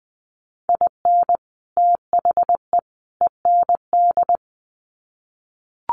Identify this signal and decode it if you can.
Morse code, heard as a keyed tone, about 20 wpm, 700 Hz. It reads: IN THE END
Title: Sets of 3 Words 20wpm